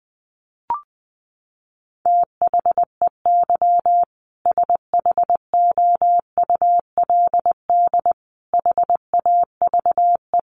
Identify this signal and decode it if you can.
Morse code, heard as a keyed tone, about 19 wpm, 700 Hz. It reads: THEY SHOULD HAVE
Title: Sentences from Top 1500 Words 20wpm